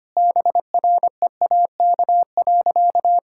Morse code, heard as a keyed tone, about 25 wpm, 700 Hz. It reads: BREAK.
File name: practice / 25wpm / Sentences with 4-Word Phrases - 25wpm